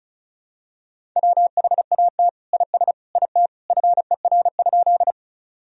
Morse code, heard as a keyed tone, about 36 wpm, 700 Hz. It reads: WHAT IS IT FER?
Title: Sentences from Top 100 Words and Abbreviations 35wpm